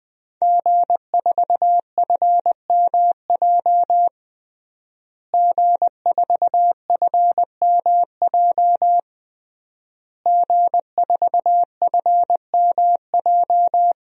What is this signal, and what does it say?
Morse code, heard as a keyed tone, about 20 wpm, 700 Hz. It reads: G4FMJ G4FMJ G4FMJ
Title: DXCC-100 Call Signs - 5 Characters - Repeated 3x - 20wpm